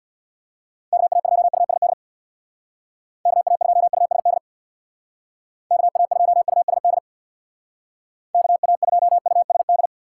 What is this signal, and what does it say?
Morse code, heard as a keyed tone, about 60 wpm, 700 Hz. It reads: KA2USD KA2USD KA2USD KA2USD
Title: Speed Racing - USA 2x3 Call Signs 50wpm